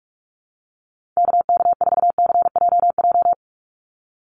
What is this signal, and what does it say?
Morse code, heard as a keyed tone, about 44 wpm, 700 Hz. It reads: KK4CJJ
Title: USA Call Signs 45wpm